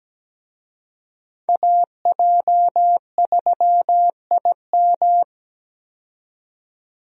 Morse code, heard as a keyed tone, about 17 wpm, 700 Hz. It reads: AJ3IM
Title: USA Call Signs - 5 Characters - 17wpm